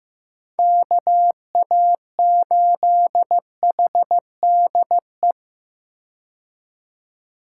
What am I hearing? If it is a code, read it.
Morse code, heard as a keyed tone, about 15 wpm, 700 Hz. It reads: KA8HDE
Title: K1USN SST QSO Practice - 15wpm